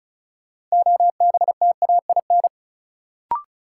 Morse code, heard as a keyed tone, about 36 wpm, 700 Hz. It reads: OBTAIN
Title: Sentences from Top 1300 Words - New Word Drill - 35wpm